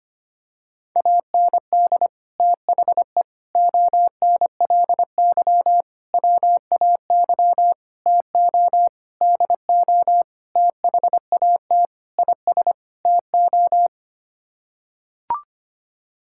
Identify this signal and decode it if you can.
Morse code, heard as a keyed tone, about 25 wpm, 700 Hz. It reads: AND THE ONLY WAY TO DO THAT IS TO
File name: Sets of 9 Words - 25wpm